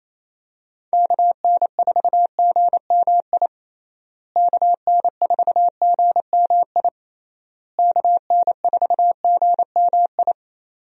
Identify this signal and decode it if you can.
Morse code, heard as a keyed tone, about 28 wpm, 700 Hz. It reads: KN4GMI KN4GMI KN4GMI
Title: USA Call Signs - 6 Characters - Repeated 3x - 28wpm